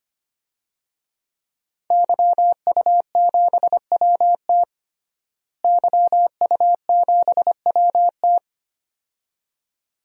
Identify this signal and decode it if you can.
Morse code, heard as a keyed tone, about 25 wpm, 700 Hz. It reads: YU7WT YU7WT
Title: DXCC-100 Call Signs - 5 Characters - Repeated 2x - 25wpm